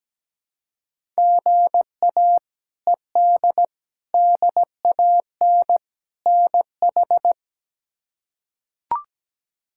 Morse code, heard as a keyed tone, about 17 wpm, 700 Hz. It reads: GA ED DAN NH
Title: K1USN SST QSO Practice - 17wpm